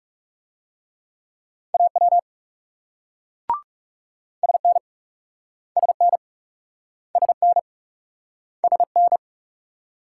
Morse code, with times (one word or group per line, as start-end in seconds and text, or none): 1.7-2.2 AW
4.4-4.8 SN
5.8-6.2 SN
7.2-7.6 SN
8.6-9.2 SN